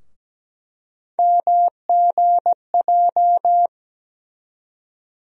Morse code, heard as a keyed tone, about 17 wpm, 700 Hz. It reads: MGJ
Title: USA Call Sign Postfixes - 3 Letters 17wpm